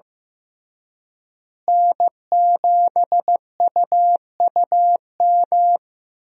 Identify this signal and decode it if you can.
Morse code, heard as a keyed tone, about 15 wpm, 700 Hz. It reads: N7UUM
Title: USA Call Signs - 5 Characters - Repeated 3x - 15wpm